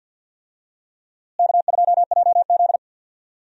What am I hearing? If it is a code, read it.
Morse code, heard as a keyed tone, about 51 wpm, 700 Hz. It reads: K2JZ
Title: USA Call Signs - 4 Characters - Repeated 3x - 50wpm